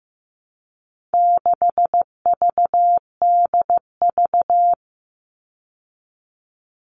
Morse code, heard as a keyed tone, about 15 wpm, 700 Hz. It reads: 6VDV
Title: All Cut Numbers - 4-Digit 15wpm